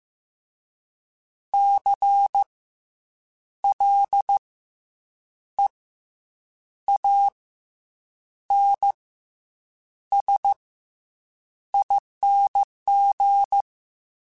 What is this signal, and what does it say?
Morse code, heard as a keyed tone, about 15 wpm, spaced as for 7 wpm, 800 Hz. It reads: CLEANSING
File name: Special INGs-mp3s / Special INGs-mp3s / Fast ing @13